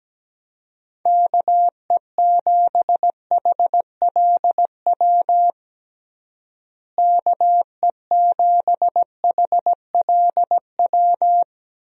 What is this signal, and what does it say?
Morse code, heard as a keyed tone, about 17 wpm, 700 Hz. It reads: KE7HLW KE7HLW
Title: USA Call Signs - 6 Characters - Repeated 2x - 17wpm